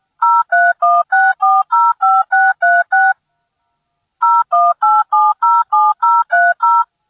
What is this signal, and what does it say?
DTMF, heard as telephone touch tones, tones of about 220 ms. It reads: 0316405636 018707030